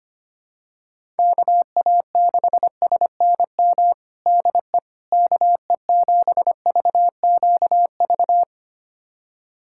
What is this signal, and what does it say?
Morse code, heard as a keyed tone, about 25 wpm, 700 Hz. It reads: KA6SNM DE KE7VQV
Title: USA Callsign DE Callsign - 6 Characters - 25wpm